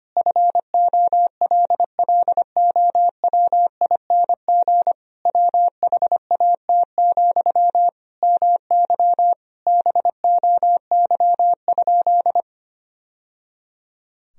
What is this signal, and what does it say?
Morse code, heard as a keyed tone, about 25 wpm, 700 Hz. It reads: FOLLOWING WHAT, MY BOY?
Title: Sentences from Top 1000 Words - Review Part 4 - 25wpm